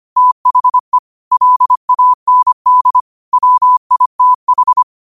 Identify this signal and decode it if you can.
Morse code, heard as a keyed tone, about 25 wpm, 1000 Hz. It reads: THE LAND WITH
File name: qotdfiles / 25wpm